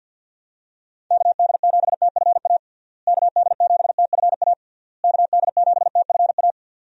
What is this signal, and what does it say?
Morse code, heard as a keyed tone, about 50 wpm, 700 Hz. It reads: KD7TFA KD7TFA KD7TFA
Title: USA Call Signs - 6 Characters - Repeated 3x - 50wpm